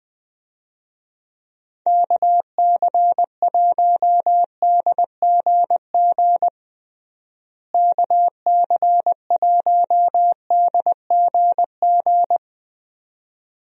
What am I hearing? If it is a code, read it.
Morse code, heard as a keyed tone, about 20 wpm, 700 Hz. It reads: KC1DGG KC1DGG
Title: USA Call Signs - 6 Characters - Repeated 2x - 20wpm